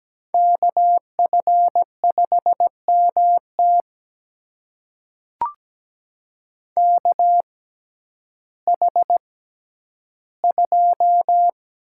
Morse code, heard as a keyed tone, about 17 wpm, spaced as for 10 wpm, 700 Hz. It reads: KF5MT KH2
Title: USA Call Signs - 5 Characters - 6x Character Spacing - 17wpm